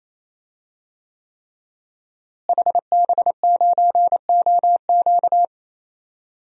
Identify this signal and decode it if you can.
Morse code, heard as a keyed tone, about 28 wpm, 700 Hz. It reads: HB9OQ
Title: DXCC-100 Call Signs - 5 Characters - Repeated 3x - 28wpm